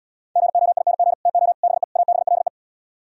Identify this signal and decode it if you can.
Morse code, heard as a keyed tone, about 77 wpm, 700 Hz. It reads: GOING TO BE ABLE